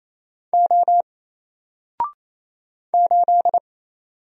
Morse code, heard as a keyed tone, about 28 wpm, 700 Hz. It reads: O 8
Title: Single Character (Letter:Number:Punctuation:Prosign) Alt 28wpm